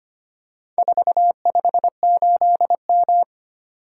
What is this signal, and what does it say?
Morse code, heard as a keyed tone, about 25 wpm, 700 Hz. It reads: 458M
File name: practice / 25wpm / Four Letter-Number Combinations - 25wpm